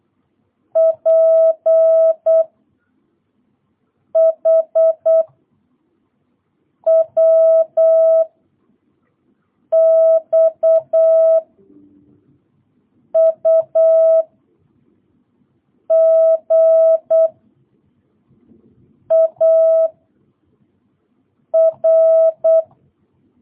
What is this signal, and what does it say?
Morse code, heard as a keyed tone, about 8 wpm, 660 Hz. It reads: P H W X U G A R